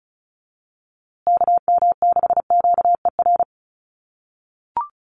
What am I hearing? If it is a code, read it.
Morse code, heard as a keyed tone, about 34 wpm, 700 Hz. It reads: KM6QER
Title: USA Call Signs Repeated 3x 35wpm